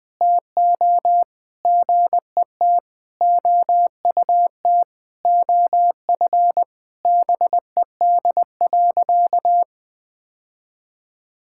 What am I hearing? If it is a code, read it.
Morse code, heard as a keyed tone, about 20 wpm, 700 Hz. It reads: TO GET OUT OF BED.
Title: Sentences with 3-Word Phrases - 20wpm